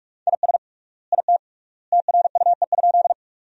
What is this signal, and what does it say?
Morse code, heard as a keyed tone, about 46 wpm, 700 Hz. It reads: IS IT TRUE?
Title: 3x Word-Spacing - Sentences from Top 400 Words - 45wpm